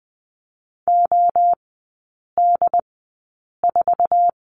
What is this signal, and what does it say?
Morse code, heard as a keyed tone, about 20 wpm, spaced as for 8 wpm, 700 Hz. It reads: OD4
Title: 087 Lesson 15e - Number 5 - Three Characters 20wpm